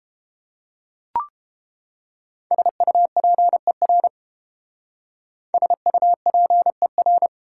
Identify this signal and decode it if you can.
Morse code, heard as a keyed tone, about 29 wpm, 700 Hz. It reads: SUPER SUPER